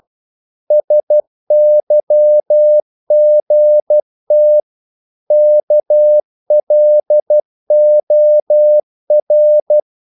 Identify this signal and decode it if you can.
Morse code, heard as a keyed tone, about 12 wpm, 600 Hz. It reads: SYGT KLOR
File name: Korte ord DK 12wpm